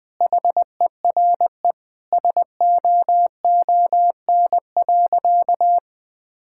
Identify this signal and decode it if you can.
Morse code, heard as a keyed tone, about 20 wpm, 700 Hz. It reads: HERE SOON.
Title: Sentences from Top 1600 Words 20wpm